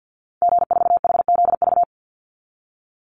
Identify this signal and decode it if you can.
Morse code, heard as a keyed tone, about 50 wpm, 700 Hz. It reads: G4HZV
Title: Call Signs Repeated 3x - 50wpm